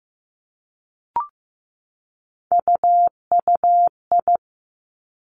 Morse code, heard as a keyed tone, about 15 wpm, 700 Hz. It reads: UUI
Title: Three Challenging Letter-Number Combinations 15wpm